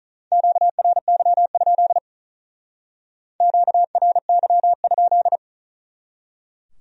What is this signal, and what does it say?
Morse code, heard as a keyed tone, about 39 wpm, 700 Hz. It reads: QRY? QRY?
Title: Speed-Racing Q-Codes - 35wpm